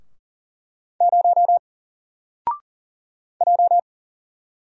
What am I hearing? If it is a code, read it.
Morse code, heard as a keyed tone, about 40 wpm, 700 Hz. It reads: 0 J
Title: Single Character (Letter:Number:Punctuation:Prosign) Alt 40wpm